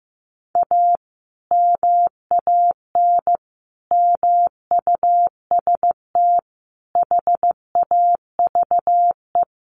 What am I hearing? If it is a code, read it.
Morse code, heard as a keyed tone, about 15 wpm, 700 Hz. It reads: A MAN MUST HAVE
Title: Sentences from Top 1200 Words 15wpm